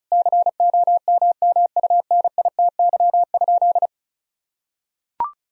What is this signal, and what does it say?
Morse code, heard as a keyed tone, about 36 wpm, 700 Hz. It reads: COMMUNITY?
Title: Sentences from Top 1000 Words - Review Part 4 - 35wpm